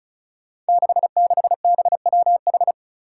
Morse code, heard as a keyed tone, about 36 wpm, 700 Hz. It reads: 66BWH
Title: Five Challenging Letter-Number Combinations 35wpm